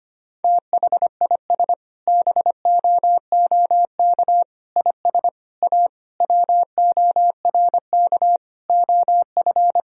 Morse code, heard as a keyed tone, about 25 wpm, 700 Hz. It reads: THIS BOOK IS A WORK OF
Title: Sentences from Top 1000 Words - Review Part 3 - 25wpm